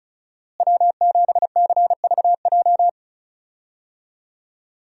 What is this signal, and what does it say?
Morse code, heard as a keyed tone, about 36 wpm, 700 Hz. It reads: W7CVJ
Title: USA Call Signs - 5 Characters - Repeated 2x - 35wpm